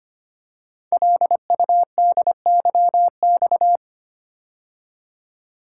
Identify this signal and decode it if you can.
Morse code, heard as a keyed tone, about 25 wpm, 700 Hz. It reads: LUDYX
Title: Code Groups - 5 Letters - 25wpm